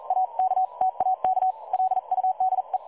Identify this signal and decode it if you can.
Morse code, heard as a keyed tone, about 43 wpm, spaced as for 38 wpm, 750 Hz. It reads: AK TAK LUDE
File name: dlazaawansowanych / nagraniazpasma / websdr recording 3565.0kHz